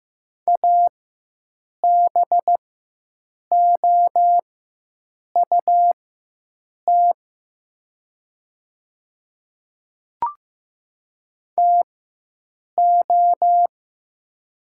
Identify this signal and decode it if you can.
Morse code, heard as a keyed tone, about 15 wpm, spaced as for 7 wpm, 700 Hz. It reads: ABOUT TO
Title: 4x Character Spacing - Top 100 Words - 15wpm